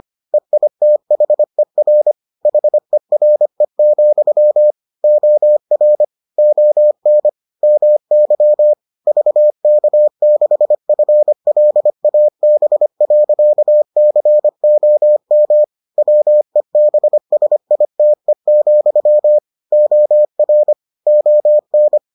Morse code, heard as a keyed tone, about 25 wpm, 600 Hz. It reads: EITHER HERE, OR ON MY VK6FLAB.COM WEBSITE, OR ON